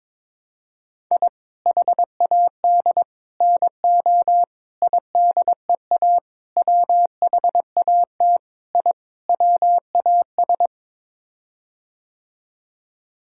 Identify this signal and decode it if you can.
Morse code, heard as a keyed tone, about 21 wpm, 700 Hz. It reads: I HAD NO IDEA WHAT I WAS
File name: practice / 22wpm / Sets of 7 Words - 22wpm